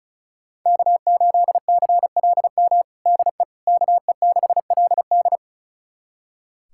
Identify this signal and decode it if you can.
Morse code, heard as a keyed tone, about 36 wpm, 700 Hz. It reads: K8CLM DE KE6LD
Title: USA Callsign DE Callsign - 5 Characters - 35wpm